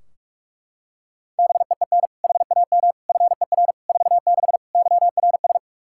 Morse code, heard as a keyed tone, about 41 wpm, 700 Hz. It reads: BEEN HAM FER 46 YRS
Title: Practice QSO - Exchange 3 - More Info - 45wpm